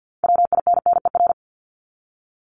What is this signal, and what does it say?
Morse code, heard as a keyed tone, about 50 wpm, 700 Hz. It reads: WINNER
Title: Top 1751-2000 Words 50wpm